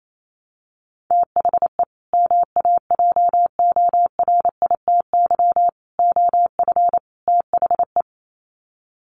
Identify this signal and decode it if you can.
Morse code, heard as a keyed tone, about 28 wpm, 700 Hz. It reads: THE MAJORITY OF THE
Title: Sets of 4 Words 28wpm